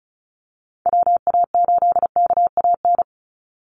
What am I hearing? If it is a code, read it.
Morse code, heard as a keyed tone, about 34 wpm, 700 Hz. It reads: WA8KAN
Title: Call Signs Repeated 3x - 35wpm